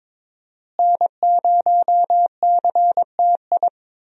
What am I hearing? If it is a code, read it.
Morse code, heard as a keyed tone, about 22 wpm, 700 Hz. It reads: N0CTI
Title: USA Call Signs - 5 Characters - Repeated 3x - 22wpm